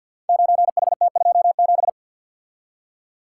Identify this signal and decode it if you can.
Morse code, heard as a keyed tone, about 45 wpm, 700 Hz. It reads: 9HT27
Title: Code Groups - 5 Letter:Numbers - 50wpm